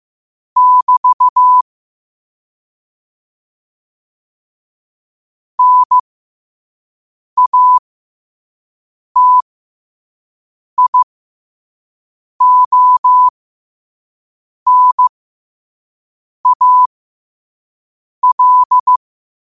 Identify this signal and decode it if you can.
Morse code, heard as a keyed tone, about 15 wpm, spaced as for 5 wpm, 1000 Hz. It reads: <BT> NATIONAL
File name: qotdfiles / 5wpm